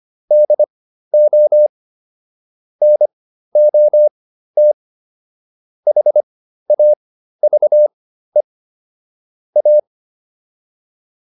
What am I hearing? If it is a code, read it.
Morse code, heard as a keyed tone, about 25 wpm, spaced as for 13 wpm, 600 Hz. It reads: DO NOT HAVE A